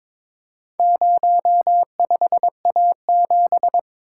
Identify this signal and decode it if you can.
Morse code, heard as a keyed tone, about 23 wpm, 700 Hz. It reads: 05A7
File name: practice / 22wpm / Four Letter-Number Combinations - 22wpm